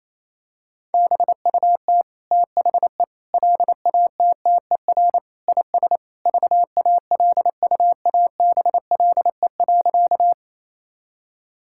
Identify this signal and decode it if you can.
Morse code, heard as a keyed tone, about 27 wpm, 700 Hz. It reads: BUT THE LATTER IS VALUABLE.
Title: Sentences from Top 1600 Words 28wpm